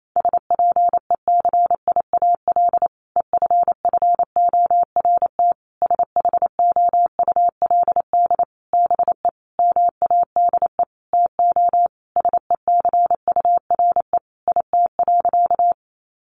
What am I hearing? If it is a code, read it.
Morse code, heard as a keyed tone, about 28 wpm, 700 Hz. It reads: SPECIAL EFFORT SHOULD BE MADE TO SECURE IT.
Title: Sentences from Top 1200 Words 28wpm